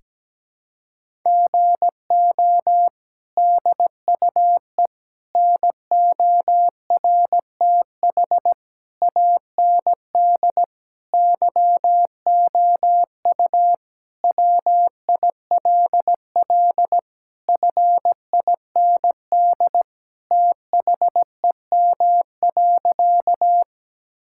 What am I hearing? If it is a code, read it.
Morse code, heard as a keyed tone, about 17 wpm, 700 Hz. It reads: GO DUE NORTH AND YOU WILL FIND THEM.
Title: Sentences from Top 1000 Words - Review Part 6 - 17wpm